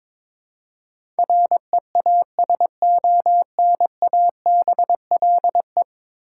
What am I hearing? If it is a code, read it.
Morse code, heard as a keyed tone, about 22 wpm, 700 Hz. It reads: REASONABLE
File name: practice / 22wpm / 10 Letter Words - First 3 Letters Given - 22wpm